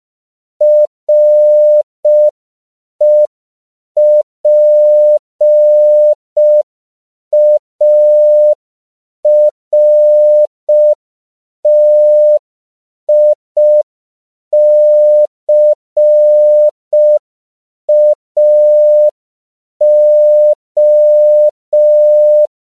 Morse code, heard as a keyed tone, about 5 wpm, 600 Hz. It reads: REPARTICAO
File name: CW / 5PPM